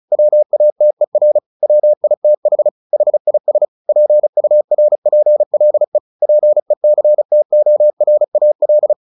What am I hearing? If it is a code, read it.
Morse code, heard as a keyed tone, about 36 wpm, 600 Hz. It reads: WATER WITH HIS PURPLE PECTORAL
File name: the old man and the sea / the old man and the sea 1 35wpm